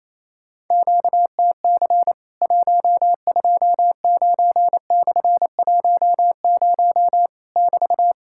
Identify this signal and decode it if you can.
Morse code, heard as a keyed tone, about 28 wpm, 700 Hz. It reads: QTC 129/10 <BT>
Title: Worked All Europe DX Contest - QTC Exchange - 28wpm